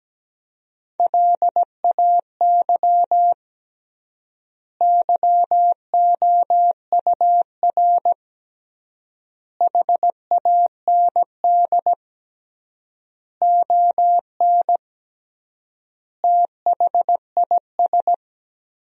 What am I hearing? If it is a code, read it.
Morse code, heard as a keyed tone, about 17 wpm, 700 Hz. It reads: LAY YOUR HAND ON THIS
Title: 3x Word-Spacing - Sentences from Top 400 Words - 17wpm